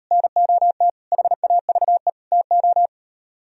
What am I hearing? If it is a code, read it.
Morse code, heard as a keyed tone, about 38 wpm, 700 Hz. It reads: NOT HAVE TO